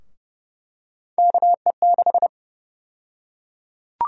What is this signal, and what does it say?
Morse code, heard as a keyed tone, about 30 wpm, 700 Hz. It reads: KE6
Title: USA Call Sign Prefixes - Repeated 2x - 30wpm